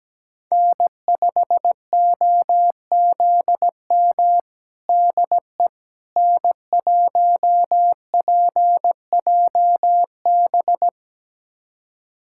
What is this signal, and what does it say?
Morse code, heard as a keyed tone, about 17 wpm, 700 Hz. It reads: N5OZM DE N1PJB
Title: USA Callsign DE Callsign - 5 Characters - 17wpm